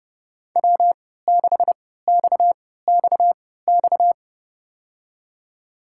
Morse code, heard as a keyed tone, about 30 wpm, spaced as for 17 wpm, 700 Hz. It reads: W6XXX
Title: USA Call Signs - 5 Characters - 3x Character Spacing - 30wpm